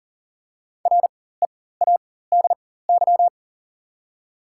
Morse code, heard as a keyed tone, about 40 wpm, spaced as for 19 wpm, 700 Hz. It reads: READY
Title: 4x Character Spacing - Top 500 Words - 40wpm